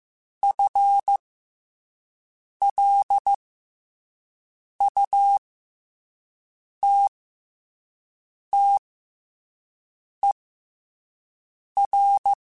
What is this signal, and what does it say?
Morse code, heard as a keyed tone, about 15 wpm, spaced as for 5 wpm, 800 Hz. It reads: FLUTTER